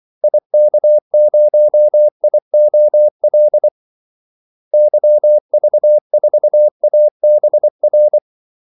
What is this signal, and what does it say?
Morse code, heard as a keyed tone, about 24 wpm, 600 Hz. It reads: IK0IOL YV4ABR
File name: Callsigns-24wpm